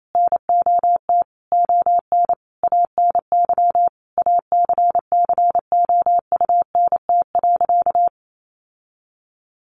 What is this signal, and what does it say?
Morse code, heard as a keyed tone, about 28 wpm, 700 Hz. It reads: NOT ON ANY ACCOUNT.
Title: Sentences from Top 900 Words 28wpm